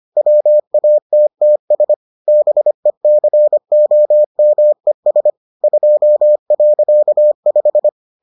Morse code, heard as a keyed tone, about 25 wpm, 600 Hz. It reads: WATTS BECOMES 2.5